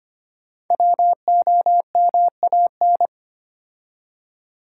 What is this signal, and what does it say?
Morse code, heard as a keyed tone, about 25 wpm, 700 Hz. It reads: WOMAN
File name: 5 Letter Words - First 3 Letters Given - 25wpm